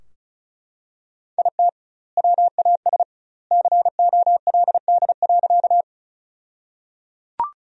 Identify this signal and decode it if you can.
Morse code, heard as a keyed tone, about 36 wpm, 700 Hz. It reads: IT WAS COLD.
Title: 2x Word-Spacing - Sentences from Top 400 Words - 35wpm